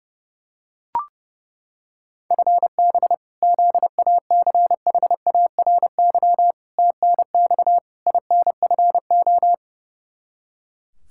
Morse code, heard as a keyed tone, about 32 wpm, 700 Hz. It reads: FB ZACHARY TNX INFO
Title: Practice QSO - Exchange 3 - More Info - 30wpm